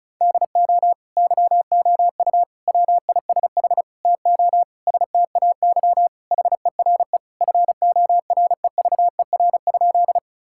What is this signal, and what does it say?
Morse code, heard as a keyed tone, about 36 wpm, 700 Hz. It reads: DO YOU WISH TO STAY HERE FOREVER?
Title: Sentences from Top 1600 Words 35wpm